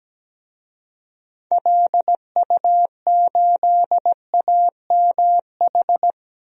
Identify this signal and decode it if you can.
Morse code, heard as a keyed tone, about 17 wpm, 700 Hz. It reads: LU8AMH
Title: DXCC-100 Call Signs - 6 Characters - Repeated 2x - 17wpm